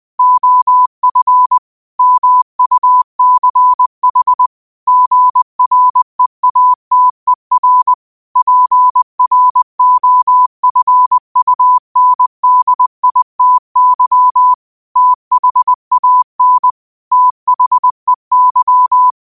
Morse code, heard as a keyed tone, about 20 wpm, 1000 Hz. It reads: OF MUCH GREATER PROFUNDITY THAN THEY